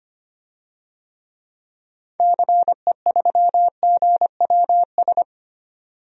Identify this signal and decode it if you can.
Morse code, heard as a keyed tone, about 25 wpm, 700 Hz. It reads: CE3GWS
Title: DXCC-100 Call Signs - 6 Characters - 25wpm